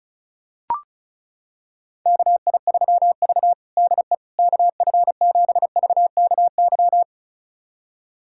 Morse code, heard as a keyed tone, about 36 wpm, 700 Hz. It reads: KI3V DE KF7VKY
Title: USA Callsign DE Callsign 35wpm